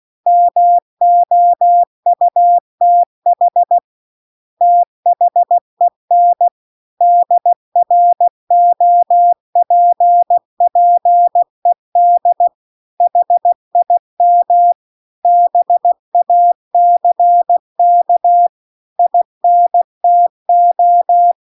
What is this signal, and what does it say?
Morse code, heard as a keyed tone, about 16 wpm, 700 Hz. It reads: MOUTH THEN DROPPED HIM BACK INTO
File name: CW-The-Big-Twoheaded-River-2-16WPM